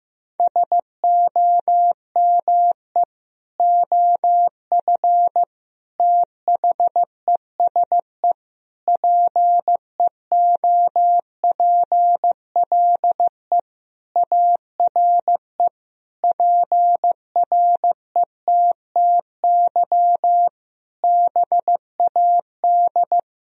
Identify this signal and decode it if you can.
Morse code, heard as a keyed tone, about 15 wpm, 700 Hz. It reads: SOME OF THESE PEOPLE ARE PRETTY BAD
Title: Sentences from Top 1000 Words - Review Part 3 - 15wpm